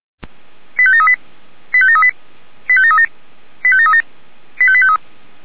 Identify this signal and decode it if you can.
Five-tone selective call (ZVEI), heard as quick runs of five tones, tones of about 70 ms. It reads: 85738 85738 85737 85737 85853
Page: /464/SDR/ZVEI/